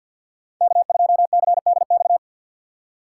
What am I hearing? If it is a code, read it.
Morse code, heard as a keyed tone, about 47 wpm, 700 Hz. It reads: K2CDX
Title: K1USN SST QSO Practice - 50wpm